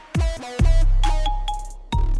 There is no code.